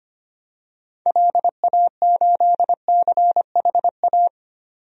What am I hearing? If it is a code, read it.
Morse code, heard as a keyed tone, about 25 wpm, 700 Hz. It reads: LA8CHA
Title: DXCC-100 Call Signs - 6 Characters - 25wpm